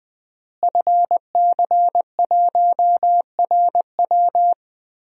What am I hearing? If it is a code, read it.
Morse code, heard as a keyed tone, about 21 wpm, 700 Hz. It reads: FC1RW
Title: DXCC-100 Call Signs - 5 Characters - Repeated 3x - 20wpm